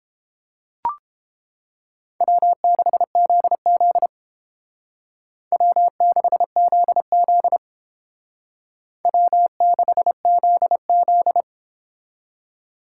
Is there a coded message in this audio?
Yes: Morse code from 2.2 to 11.4 s, heard as a keyed tone, about 31 wpm, 700 Hz. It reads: W6ZZ W6ZZ W6ZZ